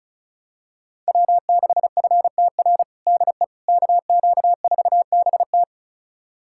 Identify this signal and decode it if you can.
Morse code, heard as a keyed tone, about 36 wpm, 700 Hz. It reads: W6FTR DE KQ4BT
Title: USA Callsign DE Callsign - 5 Characters - 35wpm